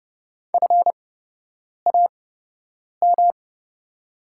Morse code, heard as a keyed tone, about 30 wpm, spaced as for 8 wpm, 700 Hz. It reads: FAM
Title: 8x Character Spacing - Top 500 Words - 30wpm